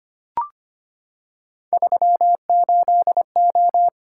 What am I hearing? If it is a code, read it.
Morse code, heard as a keyed tone, about 25 wpm, 700 Hz. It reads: 38O
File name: Three Letter-Number Combinations - 25wpm